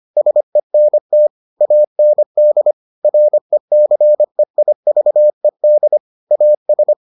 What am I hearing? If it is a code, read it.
Morse code, heard as a keyed tone, about 25 wpm, 600 Hz. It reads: SENT AND RECEIVED AS